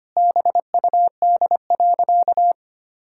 Morse code, heard as a keyed tone, about 25 wpm, 700 Hz. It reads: BUD.
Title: Pangram Challenge v2 - 25wpm